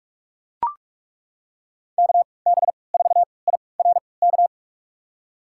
Morse code, heard as a keyed tone, about 41 wpm, spaced as for 25 wpm, 700 Hz. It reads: KB4IRK
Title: USA Call Signs - 6 Characters - 3x Character Spacing - 45wpm